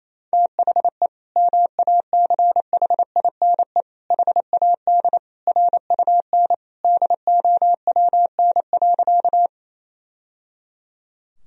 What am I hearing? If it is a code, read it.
Morse code, heard as a keyed tone, about 28 wpm, 700 Hz. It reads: THE MACHINE HAD RUN DOWN.
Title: Sentences from Top 1000 Words - Review Part 2 - 28wpm